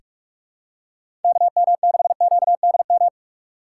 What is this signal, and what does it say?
Morse code, heard as a keyed tone, about 42 wpm, 700 Hz. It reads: KM6QDM
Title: USA Call Signs - 4x Character Spacing - 45wpm